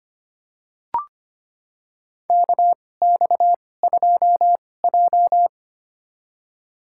Morse code, heard as a keyed tone, about 25 wpm, spaced as for 18 wpm, 700 Hz. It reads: KX2J
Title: USA Call Signs - 2x Character Spacing - 25wpm